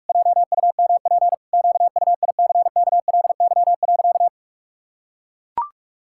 Morse code, heard as a keyed tone, about 44 wpm, 700 Hz. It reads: JUMP QUICKLY.
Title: Pangram Challenge v2 - 45wpm